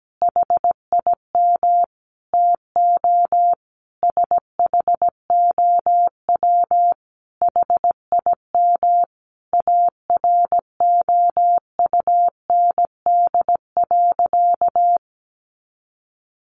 Morse code, heard as a keyed tone, about 17 wpm, 700 Hz. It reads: HIM TO SHOW HIM AROUND.
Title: Easy Phrases 17wpm